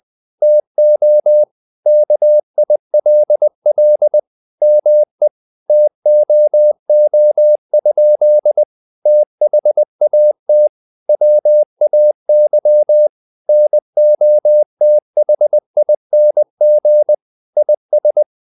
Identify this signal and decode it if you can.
Morse code, heard as a keyed tone, about 20 wpm, 600 Hz. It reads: TO KILL ME TOO? THAT WAY NOTHING IS
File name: the old man and the sea part 2 20wpm